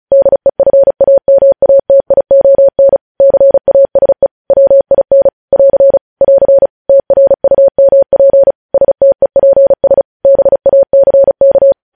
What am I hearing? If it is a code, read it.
Morse code, heard as a keyed tone, about 36 wpm, 550 Hz. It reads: DEFAMATION CASE WIN <AR> <AR> TRUMP STEPS BACK
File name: podcast / News headlines-1x-35x35